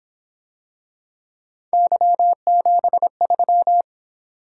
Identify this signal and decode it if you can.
Morse code, heard as a keyed tone, about 26 wpm, 700 Hz. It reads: Y73
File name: Speed-Racing - Three Letter-Number Combinations - 22wpm